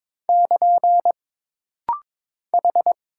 Morse code, heard as a keyed tone, about 22 wpm, 700 Hz. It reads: <KN> H
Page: This rapid-fire timing is geared to push your ICR (Instant Character Recognition) skills.